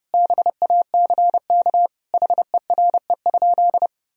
Morse code, heard as a keyed tone, about 30 wpm, 700 Hz. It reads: BACK HERE?
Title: Sentences from Top 1000 Words - Review Part 4 - 30wpm